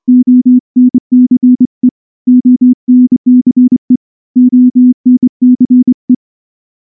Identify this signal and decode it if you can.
Morse code, heard as a keyed tone, about 24 wpm, 250 Hz. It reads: ONCE ONCE ONCE
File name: morse / morse-training-25WPM